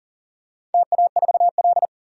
Morse code, heard as a keyed tone, about 39 wpm, 700 Hz. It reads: TA4L
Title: DXCC-100 Call Signs - 4 Characters - 40wpm